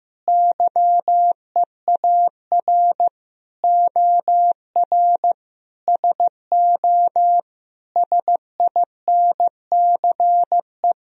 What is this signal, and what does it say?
Morse code, heard as a keyed tone, about 15 wpm, 700 Hz. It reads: YEAR OR SO SINCE
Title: Sentences from Top 1000 Words - Review Part 1 - 15wpm